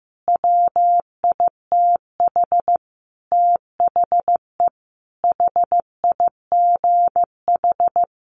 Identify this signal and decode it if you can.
Morse code, heard as a keyed tone, about 15 wpm, 700 Hz. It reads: WITH THE HIGH
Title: Sentences from Top 1400 Words 15wpm